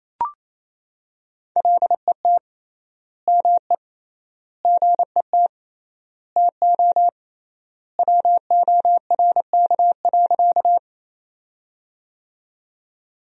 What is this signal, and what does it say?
Morse code, heard as a keyed tone, about 28 wpm, 700 Hz. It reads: LET ME GET TO WORK.
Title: 3x Word-Spacing - Sentences from Top 300 Words - 28wpm